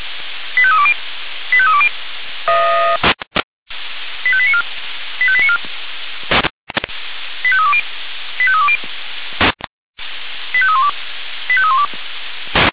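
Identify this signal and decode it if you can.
Five-tone selective call (ZVEI), heard as quick runs of five tones, tones of about 70 ms. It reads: 85320 85320 85894 85894 85320 85320 85312 85312
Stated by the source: Received on SDR, ZVEI five-tone code 85320.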